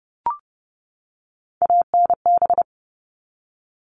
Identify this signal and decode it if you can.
Morse code, heard as a keyed tone, about 30 wpm, 700 Hz. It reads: ANB
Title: All Cut Numbers - 3-Digit 30wpm